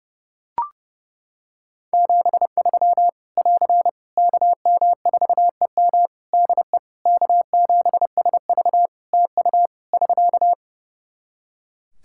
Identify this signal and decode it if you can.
Morse code, heard as a keyed tone, about 28 wpm, 700 Hz. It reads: 73 <AR> KM4EM DE K7SV TU <SK>
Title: Practice QSO - Exchange 4 - Ending - 30wpm